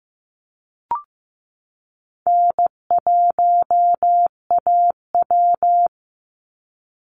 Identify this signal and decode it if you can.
Morse code, heard as a keyed tone, about 15 wpm, 700 Hz. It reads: N1AW
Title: NA SOTA Chaser Callsigns 15wpm